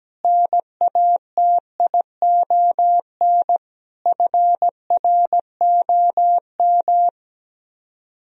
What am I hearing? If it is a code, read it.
Morse code, heard as a keyed tone, about 17 wpm, 700 Hz. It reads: NATION FROM
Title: Speed-Racing - Sets of 5 Words 17wpm